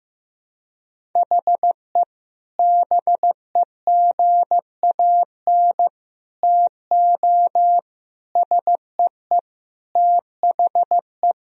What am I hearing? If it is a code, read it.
Morse code, heard as a keyed tone, about 15 wpm, 700 Hz. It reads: HE BEGAN TO SEE THE
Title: Sentences from Top 1000 Words - Review Part 1 - 15wpm